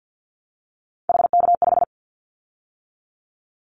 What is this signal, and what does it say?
Morse code, heard as a keyed tone, about 50 wpm, 700 Hz. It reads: HK5
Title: Callsign Prefixes for Top 100 DXCC - 50wpm